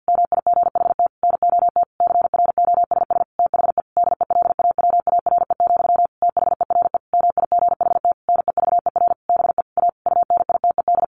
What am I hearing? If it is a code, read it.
Morse code, heard as a keyed tone, about 50 wpm, 700 Hz. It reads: MIGHT NOT CROSS THE DELAWARE, THERE MIGHT NEVER BE A UNITED
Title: A Sound Of Thunder 50wpm